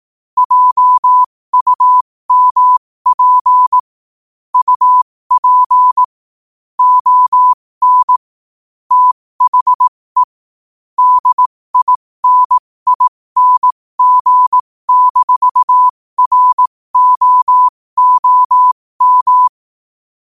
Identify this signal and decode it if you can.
Morse code, heard as a keyed tone, about 18 wpm, spaced as for 15 wpm, 1000 Hz. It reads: JUMP UP ON THE DINING-ROOM
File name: qotdfiles / 15wpm